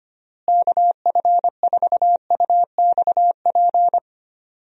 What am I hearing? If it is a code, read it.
Morse code, heard as a keyed tone, about 25 wpm, 700 Hz. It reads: KF4UXP
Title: USA Call Signs - 6 Characters - Repeated 3x - 25wpm